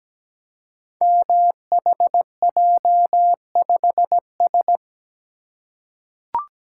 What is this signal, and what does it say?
Morse code, heard as a keyed tone, about 17 wpm, 700 Hz. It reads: MHJ5S
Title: Code Groups - 5 Letter:Numbers - 17wpm